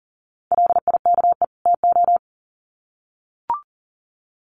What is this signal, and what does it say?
Morse code, heard as a keyed tone, about 40 wpm, 700 Hz. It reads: LIKE TO
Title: Sets of 2 Words 40wpm